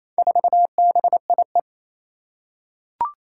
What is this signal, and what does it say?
Morse code, heard as a keyed tone, about 27 wpm, 700 Hz. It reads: 4BIE
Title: Four Challenging Letter-Number Combinations 28wpm